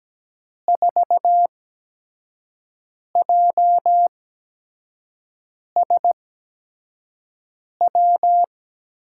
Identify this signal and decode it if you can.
Morse code, heard as a keyed tone, about 17 wpm, 700 Hz. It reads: 4 J S W
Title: USA Call Signs - 8x Character Spacing - 17wpm